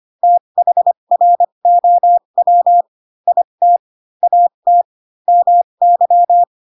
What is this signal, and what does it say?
Morse code, heard as a keyed tone, about 25 wpm, spaced as for 22 wpm, 700 Hz. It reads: THROW IT AT MY
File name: Pinocchio / Pinocchio ch2 18 25 ik1hge